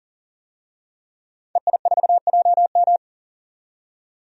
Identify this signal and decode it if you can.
Morse code, heard as a keyed tone, about 40 wpm, 700 Hz. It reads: EI4JM
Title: DXCC-100 Call Signs - 5 Characters - 40wpm